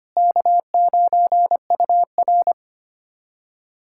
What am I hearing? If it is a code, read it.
Morse code, heard as a keyed tone, about 25 wpm, 700 Hz. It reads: K9UR
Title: USA Callsign DE Callsign - 4 Characters - 25wpm